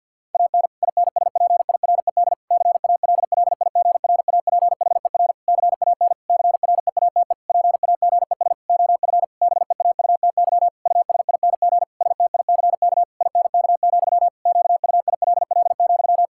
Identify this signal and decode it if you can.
Morse code, heard as a keyed tone, about 50 wpm, 700 Hz. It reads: AN INSPIRED CALLIGRAPHER CAN CREATE PAGES OF BEAUTY USING STICK INK, QUILL,
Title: Pangram Challenge v2 - 50wpm